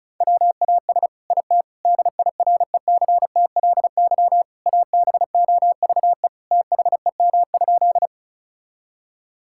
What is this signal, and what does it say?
Morse code, heard as a keyed tone, about 36 wpm, 700 Hz. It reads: WAS IT DIRECTLY ABOVE THEM?
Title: Sentences from Top 1000 Words - Review Part 5 - 35wpm